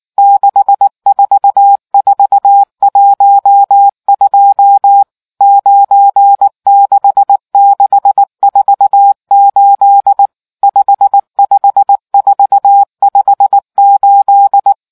Morse code, heard as a keyed tone, about 19 wpm, 800 Hz. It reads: 64412 96648 55458